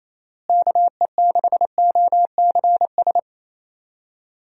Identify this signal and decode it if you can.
Morse code, heard as a keyed tone, about 28 wpm, 700 Hz. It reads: KE6OCS
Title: USA Call Signs - 6 Characters - Repeated 3x - 28wpm